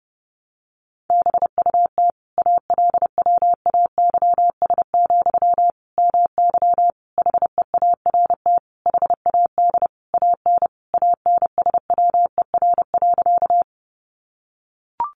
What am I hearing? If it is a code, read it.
Morse code, heard as a keyed tone, about 30 wpm, 700 Hz. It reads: BUT ALWAYS, MY HEART HAD AN ANSWER.
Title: Sentences from Top 1200 Words 30wpm